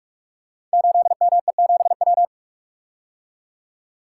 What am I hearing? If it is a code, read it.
Morse code, heard as a keyed tone, about 44 wpm, 700 Hz. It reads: 8ME7W
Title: Code Groups - 5 Letter:Numbers - 45wpm